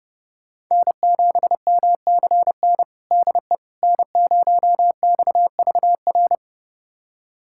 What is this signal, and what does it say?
Morse code, heard as a keyed tone, about 30 wpm, 700 Hz. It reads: N7MCN DE N0XVR
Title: USA Callsign DE Callsign - 5 Characters - 30wpm